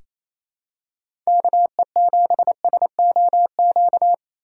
Morse code, heard as a keyed tone, about 28 wpm, 700 Hz. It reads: KE7SOQ
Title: USA Callsign DE Callsign 28wpm